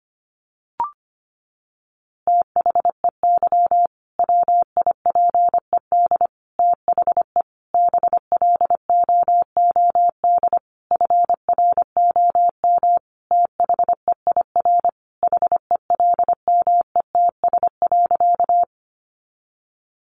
Morse code, heard as a keyed tone, about 25 wpm, 700 Hz. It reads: THEY WIPED THE BLOOD FROM THEIR HELMETS.
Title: A Sound Of Thunder 25wpm